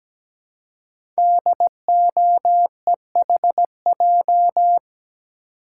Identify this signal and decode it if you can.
Morse code, heard as a keyed tone, about 17 wpm, 700 Hz. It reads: DOEHJ
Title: Code Groups - 5 Letters - 17wpm